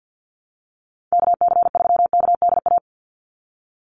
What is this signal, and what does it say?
Morse code, heard as a keyed tone, about 50 wpm, 700 Hz. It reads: KC3KDA
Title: USA Call Signs 50wpm